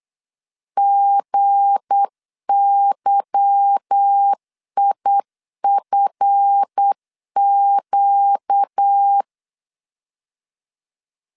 Morse code, heard as a keyed tone, about 8 wpm, 800 Hz. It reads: GYIFQ
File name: LEZIONI / 8 YJ45 / 8 Y 8wpm 3245